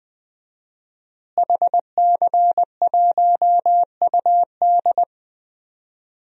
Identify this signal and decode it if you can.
Morse code, heard as a keyed tone, about 21 wpm, 700 Hz. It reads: HC1UD
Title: DXCC-100 Call Signs - 5 Characters - Repeated 3x - 20wpm